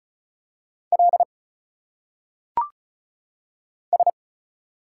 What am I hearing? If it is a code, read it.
Morse code, heard as a keyed tone, about 36 wpm, 700 Hz. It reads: L S
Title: Single Letters - Phonetic Pronunciation 35wpm